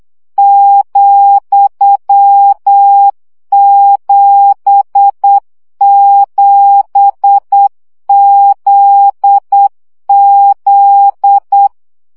Morse code, heard as a keyed tone, about 8 wpm, 800 Hz. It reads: ,77ZZ
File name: CW gyakorlas z 7 vesszo 7 baud